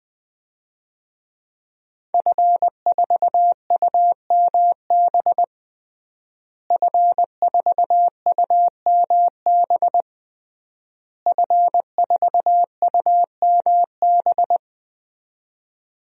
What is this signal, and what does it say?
Morse code, heard as a keyed tone, about 20 wpm, 700 Hz. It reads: F4UMB F4UMB F4UMB
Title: DXCC-100 Call Signs - 5 Characters - Repeated 3x - 20wpm